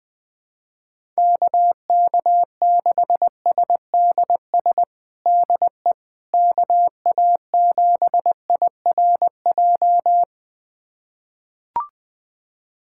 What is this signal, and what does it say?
Morse code, heard as a keyed tone, about 20 wpm, 700 Hz. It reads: KK6SDS DE KA7IRJ
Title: USA Callsign DE Callsign - 6 Characters - 20wpm